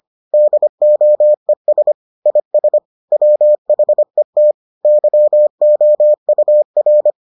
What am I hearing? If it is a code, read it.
Morse code, heard as a keyed tone, about 25 wpm, 600 Hz. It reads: DOES IS WHET YOUR